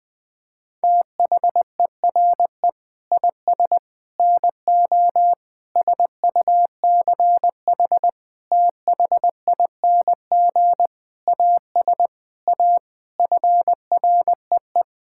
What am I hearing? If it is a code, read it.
Morse code, heard as a keyed tone, about 20 wpm, 700 Hz. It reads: THERE IS NO SUCH THING AS A FREE
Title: Sets of 9 Words - 20wpm